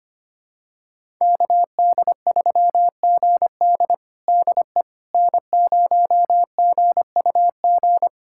Morse code, heard as a keyed tone, about 25 wpm, 700 Hz. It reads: KD3GD DE N0GUG
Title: USA Callsign DE Callsign - 5 Characters - 25wpm